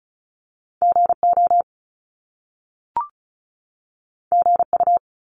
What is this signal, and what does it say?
Morse code, heard as a keyed tone, about 36 wpm, 700 Hz. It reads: GO GU
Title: Top Two Letter Combinations in Words 35wpm